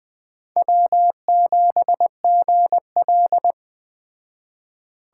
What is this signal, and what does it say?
Morse code, heard as a keyed tone, about 20 wpm, 700 Hz. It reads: W7GL
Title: Speed Racing - USA 1x2 Call Signs 17wpm